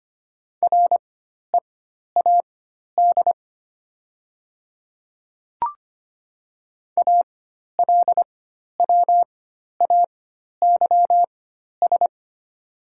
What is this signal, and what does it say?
Morse code, heard as a keyed tone, about 25 wpm, spaced as for 12 wpm, 700 Hz. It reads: READ ALWAYS
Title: 4x Character Spacing - Top 300 Words - 25wpm